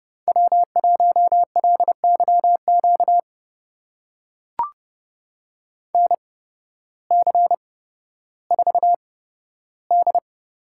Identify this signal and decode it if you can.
Morse code, heard as a keyed tone, about 30 wpm, spaced as for 14 wpm, 700 Hz. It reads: W1LYQ NC4D
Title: USA Call Signs - 8x Character Spacing - 30wpm